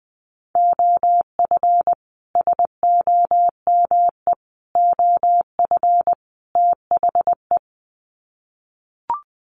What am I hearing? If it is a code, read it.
Morse code, heard as a keyed tone, about 20 wpm, 700 Hz. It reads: OF SOME OF THE
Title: Sets of 4 Words 20wpm